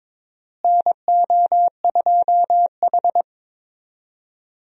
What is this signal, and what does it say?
Morse code, heard as a keyed tone, about 22 wpm, 700 Hz. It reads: NO2H
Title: USA Call Signs - 4 Characters - Repeated 2x - 22wpm